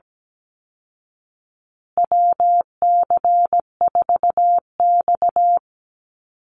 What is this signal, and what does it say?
Morse code, heard as a keyed tone, about 17 wpm, 700 Hz. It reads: WC4X
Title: Call Signs Repeated 2x - 17wpm